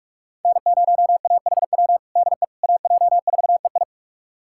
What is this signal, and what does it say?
Morse code, heard as a keyed tone, about 46 wpm, 700 Hz. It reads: N0AHW DE AJ4EI
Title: USA Callsign DE Callsign - 5 Characters - 45wpm